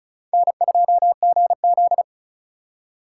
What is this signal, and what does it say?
Morse code, heard as a keyed tone, about 36 wpm, 700 Hz. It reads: N2GZ
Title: USA Call Signs - 3x Character Spacing - 35wpm